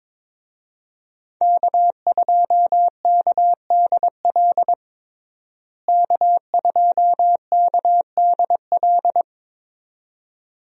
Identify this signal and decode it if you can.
Morse code, heard as a keyed tone, about 22 wpm, 700 Hz. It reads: K2KDL K2KDL
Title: USA Call Signs - 5 Characters - Repeated 2x - 22wpm